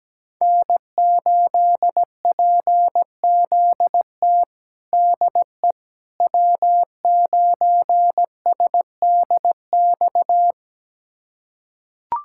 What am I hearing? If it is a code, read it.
Morse code, heard as a keyed tone, about 17 wpm, 700 Hz. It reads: N8PZT DE W9SDX
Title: USA Callsign DE Callsign - 5 Characters - 17wpm